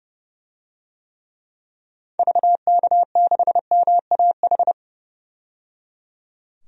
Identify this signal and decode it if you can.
Morse code, heard as a keyed tone, about 30 wpm, 700 Hz. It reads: VK6MAH
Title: DXCC-100 Call Signs - 6 Characters - 30wpm